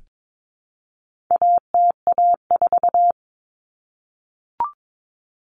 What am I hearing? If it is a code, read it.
Morse code, heard as a keyed tone, about 22 wpm, 700 Hz. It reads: ATA4
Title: Common Cut Numbers - 4-Digit 22wpm